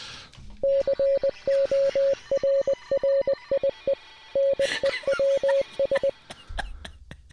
Morse code, heard as a keyed tone, about 20 wpm, 530 Hz. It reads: CORRIE NEWS